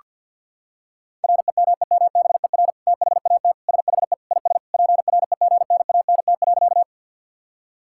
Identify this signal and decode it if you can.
Morse code, heard as a keyed tone, about 52 wpm, 700 Hz. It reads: REMEMBER THAT SHE IS PREGNANT.
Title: Sentences from Top 1000 Words - Review Part 6 - 50wpm